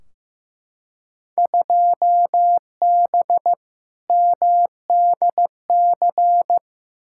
Morse code, heard as a keyed tone, about 15 wpm, 700 Hz. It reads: 2B MDC
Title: Field Day Exchanges - 15wpm